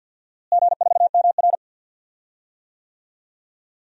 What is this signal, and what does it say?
Morse code, heard as a keyed tone, about 52 wpm, 700 Hz. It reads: G4MR
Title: DXCC-100 Call Signs - 4 Characters - Repeated 3x - 50wpm